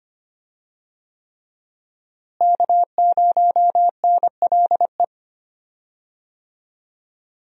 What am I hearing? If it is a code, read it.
Morse code, heard as a keyed tone, about 25 wpm, 700 Hz. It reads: K0NLE
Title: USA Call Signs - 5 Characters - Repeated 3x - 25wpm